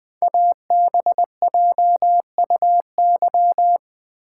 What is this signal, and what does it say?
Morse code, heard as a keyed tone, about 20 wpm, 700 Hz. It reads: ABJUY
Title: Code Groups - 5 Letters - 20wpm